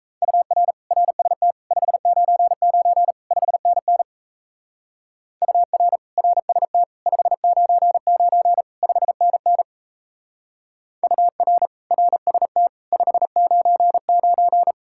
Morse code, heard as a keyed tone, about 34 wpm, 700 Hz. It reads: UR RST 599 5NN UR RST 599 5NN UR RST 599
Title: Speed-Racing Common QSO Phrases Encore - 28wpm